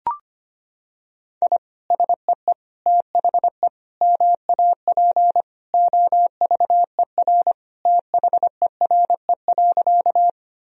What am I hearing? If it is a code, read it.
Morse code, heard as a keyed tone, about 25 wpm, 700 Hz. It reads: I SEE THE MAP OVER THERE.
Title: Sentences from Top 1000 Words - Review Part 2 - 25wpm